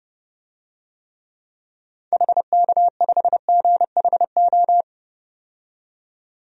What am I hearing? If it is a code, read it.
Morse code, heard as a keyed tone, about 32 wpm, 700 Hz. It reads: HK5GHO
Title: DXCC-100 Call Signs - 6 Characters - 30wpm